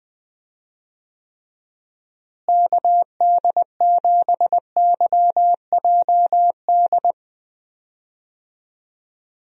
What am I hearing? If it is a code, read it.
Morse code, heard as a keyed tone, about 20 wpm, 700 Hz. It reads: KD7YJD
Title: USA Call Signs - 6 Characters - Repeated 2x - 20wpm